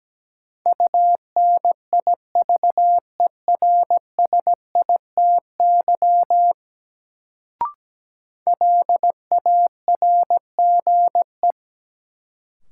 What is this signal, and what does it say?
Morse code, heard as a keyed tone, about 17 wpm, 700 Hz. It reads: UNIVERSITY LARGE
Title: Sets of 4 Words - New Word Drill - 17wpm